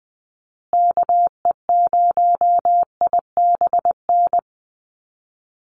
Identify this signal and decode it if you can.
Morse code, heard as a keyed tone, about 20 wpm, 700 Hz. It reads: KE0IBN
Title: USA Call Signs Repeated 3x 20wpm